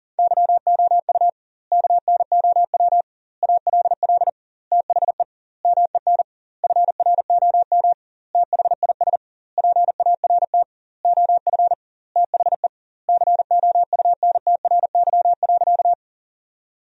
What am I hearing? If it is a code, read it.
Morse code, heard as a keyed tone, about 40 wpm, 700 Hz. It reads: YOU KNOW ALL THE MEN FROM THIS PART OF THE COUNTRY.
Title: 2x Word-Spacing - Sentences from Top 300 Words - 40wpm